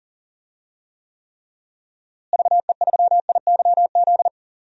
Morse code, heard as a keyed tone, about 40 wpm, 700 Hz. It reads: VE3IYZ
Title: DXCC-100 Call Signs - 6 Characters - Repeated 3x - 40wpm